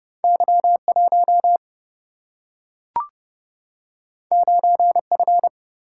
Morse code, heard as a keyed tone, about 30 wpm, 700 Hz. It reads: Y1 9F
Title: Single-Letter Single-Number 30wpm